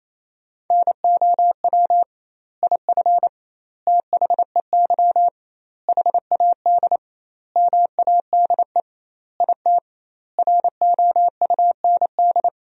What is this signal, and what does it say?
Morse code, heard as a keyed tone, about 27 wpm, 700 Hz. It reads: NOW IF THEY HAD MADE IT ROUND
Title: 2x Word-Spacing - Sentences from Top 500 Words - 28wpm